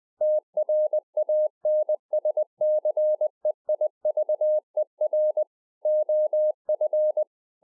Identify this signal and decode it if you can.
Morse code, heard as a keyed tone, about 20 wpm, 610 Hz. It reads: TRANSCEIVER OF
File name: 2011-04-mp3 / A SMALL TRANCEIVER 3 20WPM